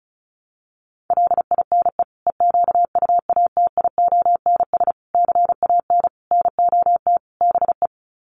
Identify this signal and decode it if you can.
Morse code, heard as a keyed tone, about 34 wpm, 700 Hz. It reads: LINE EQUATIONS CAN NOT BE
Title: Sets of 5 Words 35wpm